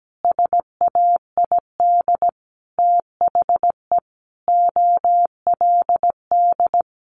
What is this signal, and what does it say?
Morse code, heard as a keyed tone, about 17 wpm, 700 Hz. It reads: SAID THE OLD
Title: Sentences from Top 1500 Words 17wpm